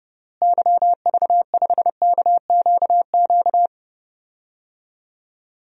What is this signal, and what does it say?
Morse code, heard as a keyed tone, about 30 wpm, 700 Hz. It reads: YV5KQQ
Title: DXCC-100 Call Signs - 6 Characters - Repeated 3x - 30wpm